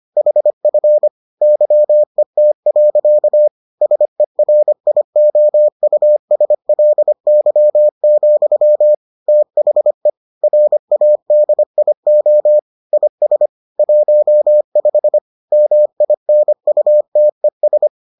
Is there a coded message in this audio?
Yes: Morse code from 0.2 to 17.9 s, heard as a keyed tone, about 25 wpm, 600 Hz. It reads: HF YET. SERIOUSLY, THE RADIO IS 15 MINUTES